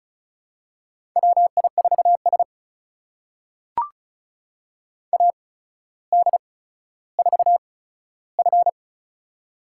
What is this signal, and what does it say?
Morse code, heard as a keyed tone, about 36 wpm, spaced as for 15 wpm, 700 Hz. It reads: WI4S AD4F
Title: USA Call Signs - 4 Characters - 8x Character Spacing - 35wpm